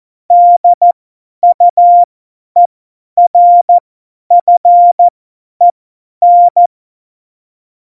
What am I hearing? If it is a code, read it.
Morse code, heard as a keyed tone, about 14 wpm, spaced as for 10 wpm, 700 Hz. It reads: DUERFEN
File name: Morsekurs / Witze / Witz 13 March 2018, 50 F70